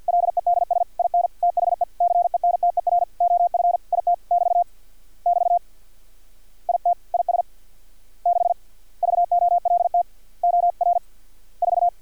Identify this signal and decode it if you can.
Morse code, heard as a keyed tone, about 50 wpm, 700 Hz. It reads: LEDS IN THE CENTER OF IT <BT> <BT> IT IS 6 VOLT OR 4
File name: archived / CRSnet